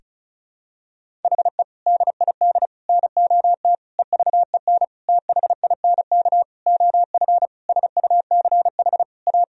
Morse code, heard as a keyed tone, about 34 wpm, 700 Hz. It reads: HE DID NOT EVEN THINK OF SUCH A
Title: Sentences from Top 1000 Words - Review Part 1 - 35wpm